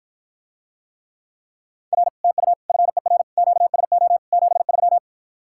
Morse code, heard as a keyed tone, about 54 wpm, 700 Hz. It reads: R TU FER QSO 73